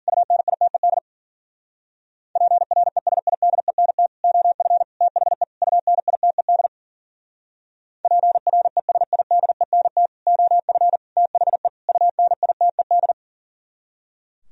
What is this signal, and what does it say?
Morse code, heard as a keyed tone, about 52 wpm, 700 Hz. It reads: UNITED PRESIDENT OF THE UNITED PRESIDENT OF THE UNITED